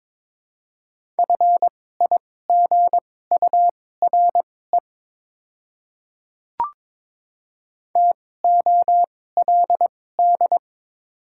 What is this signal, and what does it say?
Morse code, heard as a keyed tone, about 22 wpm, spaced as for 16 wpm, 700 Hz. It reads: FIGURE TOLD
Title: 2x Character Spacing - Top 500 Words - 22wpm